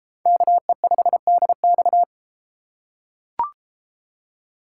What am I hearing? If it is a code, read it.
Morse code, heard as a keyed tone, about 32 wpm, 700 Hz. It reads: KE5DX